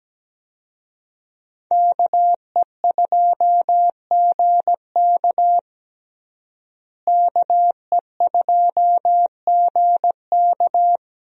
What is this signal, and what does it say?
Morse code, heard as a keyed tone, about 17 wpm, 700 Hz. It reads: KE2GK KE2GK
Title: USA Call Signs - 5 Characters - Repeated 3x - 17wpm